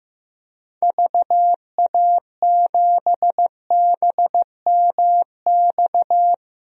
Morse code, heard as a keyed tone, about 15 wpm, 700 Hz. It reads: VA7BMX
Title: DXCC-100 Call Signs - 6 Characters - Repeated 3x - 15wpm